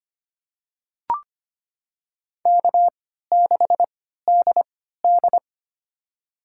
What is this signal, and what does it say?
Morse code, heard as a keyed tone, about 25 wpm, spaced as for 14 wpm, 700 Hz. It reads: K6DD
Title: USA Call Signs - 4 Characters - 3x Character Spacing - 25wpm